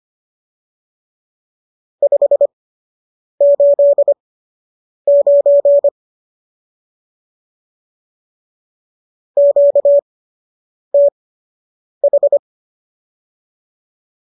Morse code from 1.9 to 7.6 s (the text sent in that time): 589